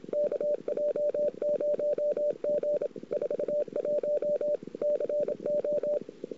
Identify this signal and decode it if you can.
Morse code, heard as a keyed tone, about 27 wpm, 590 Hz. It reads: KJ0G 41 CO
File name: CW / 2008-ss-cw-k5tr-op-n5rz -16:00